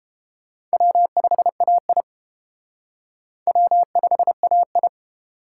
Morse code, heard as a keyed tone, about 32 wpm, 700 Hz. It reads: W5AI W5AI